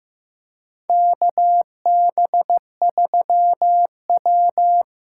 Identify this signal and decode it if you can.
Morse code, heard as a keyed tone, about 15 wpm, 700 Hz. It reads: KB3W
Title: USA Call Signs - 6 Characters - 6x Character Spacing - 15wpm